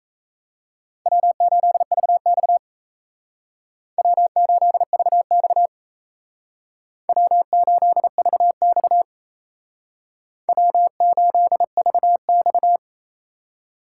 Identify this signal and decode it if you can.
Morse code, heard as a keyed tone, about 35 wpm, 700 Hz. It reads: W8VX W8VX W8VX W8VX
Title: Speed Racing - USA 1x2 Call Signs 28wpm